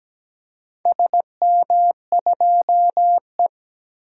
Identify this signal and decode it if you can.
Morse code, heard as a keyed tone, about 17 wpm, 700 Hz. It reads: SM2E
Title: DXCC-100 Call Signs - 4 Characters - Repeated 3x - 17wpm